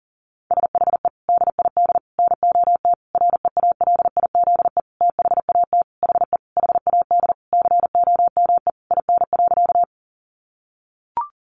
Morse code, heard as a keyed tone, about 37 wpm, 700 Hz. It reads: SHE DID NOT REALIZE THAT HE HAD COME IN.
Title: Sentences from Top 1000 Words 40wpm